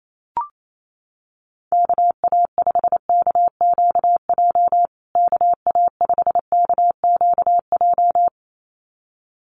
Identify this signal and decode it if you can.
Morse code, heard as a keyed tone, about 28 wpm, 700 Hz. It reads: KA5KQJ KA5KQJ
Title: USA Call Signs Repeated 2x 28wpm